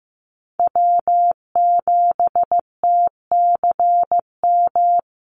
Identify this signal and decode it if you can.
Morse code, heard as a keyed tone, about 15 wpm, 700 Hz. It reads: W7TCM
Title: USA Call Signs Repeated 2x 15wpm